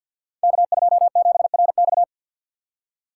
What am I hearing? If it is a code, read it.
Morse code, heard as a keyed tone, about 52 wpm, 700 Hz. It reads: K27RX
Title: Code Groups - 5 Letter:Numbers - 50wpm